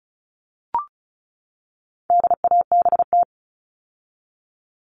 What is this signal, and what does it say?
Morse code, heard as a keyed tone, about 34 wpm, 700 Hz. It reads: DABT
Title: All Cut Numbers - 4-Digit 35wpm